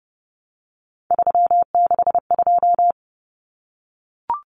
Morse code, heard as a keyed tone, about 30 wpm, 700 Hz. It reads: 362
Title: Common Cut Numbers - 3-Digit 30wpm